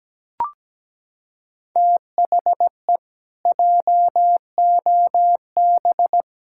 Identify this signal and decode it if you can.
Morse code, heard as a keyed tone, about 17 wpm, 700 Hz. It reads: THE JOB
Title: Pangram Challenge v2 - 17wpm